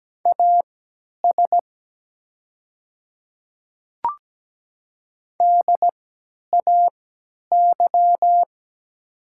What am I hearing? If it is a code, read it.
Morse code, heard as a keyed tone, about 17 wpm, spaced as for 10 wpm, 700 Hz. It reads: AS DAY
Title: 3x Character Spacing - Top 200 Words - 17wpm